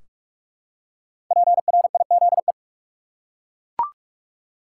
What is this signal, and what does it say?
Morse code, heard as a keyed tone, about 41 wpm, 700 Hz. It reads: PRIZE
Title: Five Letter Words - 45wpm